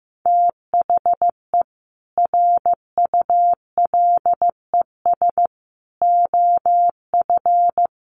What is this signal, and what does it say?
Morse code, heard as a keyed tone, about 15 wpm, 700 Hz. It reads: THE RULES OF
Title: Sets of 5 Words Encore2 15wpm